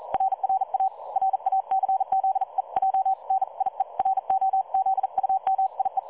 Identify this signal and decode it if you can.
Morse code, heard as a keyed tone, about 40 wpm, 770 Hz. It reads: RRA RACZEJ NIE ROZUMIE
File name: dlazaawansowanych / nagraniazpasma / websdr recording 3565.0kHz